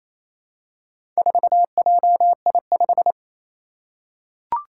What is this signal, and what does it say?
Morse code, heard as a keyed tone, about 28 wpm, 700 Hz. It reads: 4JI5
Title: Four Challenging Letter-Number Combinations 28wpm